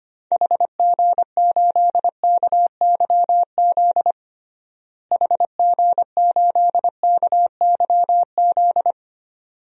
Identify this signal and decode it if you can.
Morse code, heard as a keyed tone, about 25 wpm, 700 Hz. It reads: HG8KYZ HG8KYZ
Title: DXCC-100 Call Signs - 6 Characters - Repeated 2x - 25wpm